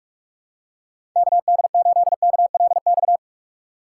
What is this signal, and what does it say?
Morse code, heard as a keyed tone, about 50 wpm, 700 Hz. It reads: KD8KLX